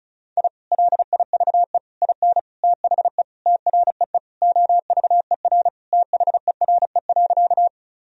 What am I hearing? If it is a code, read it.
Morse code, heard as a keyed tone, about 36 wpm, 700 Hz. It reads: I LIVE IN THE TREE OVER THERE.
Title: Sentences from Top 1000 Words - Review Part 1 - 35wpm